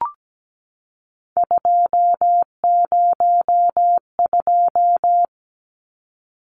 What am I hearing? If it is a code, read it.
Morse code, heard as a keyed tone, about 17 wpm, 700 Hz. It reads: 202
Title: Numbers - 3-Digit 17wpm